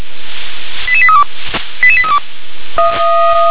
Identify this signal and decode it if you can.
Five-tone selective call (ZVEI), heard as quick runs of five tones, tones of about 80 ms. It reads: 88832 88832